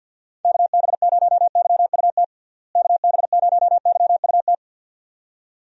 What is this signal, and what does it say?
Morse code, heard as a keyed tone, about 50 wpm, 700 Hz. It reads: KB0YUT KB0YUT
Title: USA Call Signs - 6 Characters - Repeated 2x - 50wpm